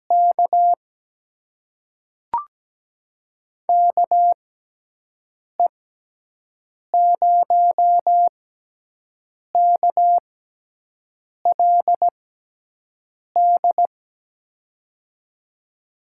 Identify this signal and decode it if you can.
Morse code, heard as a keyed tone, about 17 wpm, spaced as for 6 wpm, 700 Hz. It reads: K KE0KLD
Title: USA Call Signs - 6 Characters - 6x Character Spacing - 17wpm